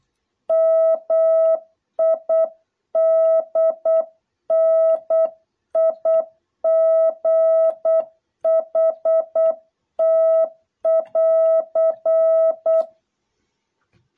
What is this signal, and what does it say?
Morse code, heard as a keyed tone, about 8 wpm, 660 Hz. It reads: MIDNIGHT<AR>